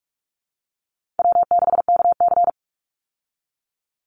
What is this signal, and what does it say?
Morse code, heard as a keyed tone, about 44 wpm, 700 Hz. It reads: W6KC
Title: Easy Call Signs Repeated 3x - 45wpm